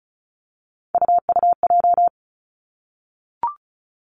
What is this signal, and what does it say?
Morse code, heard as a keyed tone, about 36 wpm, 700 Hz. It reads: UUJ
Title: Three Challenging Letter-Number Combinations 35wpm